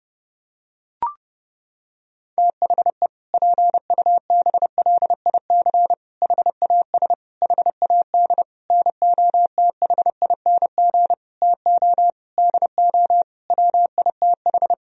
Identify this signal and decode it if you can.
Morse code, heard as a keyed tone, about 30 wpm, 700 Hz. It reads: THE PUBLIC HAS HAD NOTHING TO DO WITH
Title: Sentences from Top 1000 Words - Review Part 3 - 30wpm